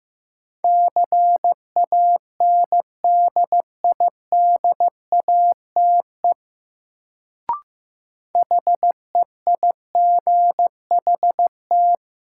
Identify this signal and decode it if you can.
Morse code, heard as a keyed tone, about 15 wpm, 700 Hz. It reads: CANDIDATE HEIGHT
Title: Sentences from Top 1600 Words - New Word Drill Alt - 15wpm